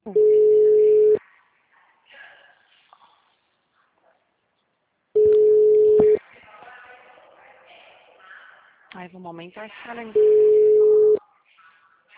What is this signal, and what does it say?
Ringback tone (Europe, 425 Hz), with speech.